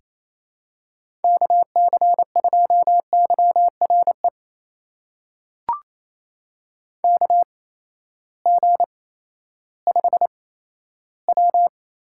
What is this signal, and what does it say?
Morse code, heard as a keyed tone, about 28 wpm, spaced as for 14 wpm, 700 Hz. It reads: KC2YRE KG5W
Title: USA Call Signs - 6 Characters - 8x Character Spacing - 28wpm